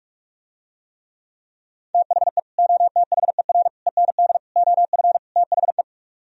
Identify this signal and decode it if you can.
Morse code, heard as a keyed tone, about 45 wpm, 700 Hz. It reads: THE OTHER END OF THE